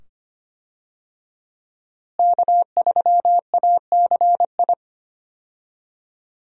Morse code, heard as a keyed tone, about 25 wpm, 700 Hz. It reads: K3ACI
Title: USA Call Signs - 5 Characters - Repeated 2x - 25wpm